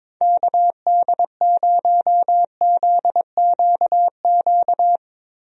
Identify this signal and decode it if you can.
Morse code, heard as a keyed tone, about 22 wpm, 700 Hz. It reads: KD0ZQQ
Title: USA Call Signs - 6 Characters - Repeated 2x - 22wpm